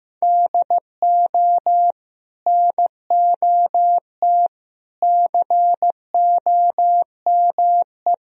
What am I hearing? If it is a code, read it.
Morse code, heard as a keyed tone, about 15 wpm, 700 Hz. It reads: DO NOT COME
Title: Sentences from Top 1000 Words - Review Part 1 - 15wpm